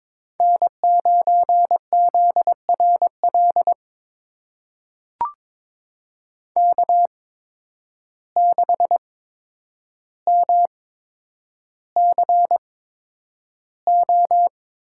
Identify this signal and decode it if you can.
Morse code, heard as a keyed tone, about 22 wpm, spaced as for 9 wpm, 700 Hz. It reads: N9ZRL K6MCO
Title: USA Call Signs - 5 Characters - 8x Character Spacing - 22wpm